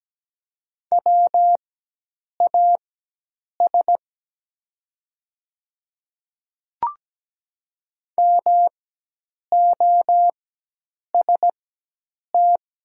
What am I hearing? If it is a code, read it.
Morse code, heard as a keyed tone, about 17 wpm, spaced as for 8 wpm, 700 Hz. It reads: WAS MOST
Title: 4x Character Spacing - Top 500 Words - 17wpm